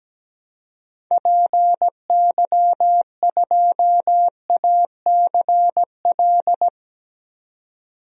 Morse code, heard as a keyed tone, about 17 wpm, 700 Hz. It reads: PY2ACL
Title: DXCC-100 Call Signs - 6 Characters - 17wpm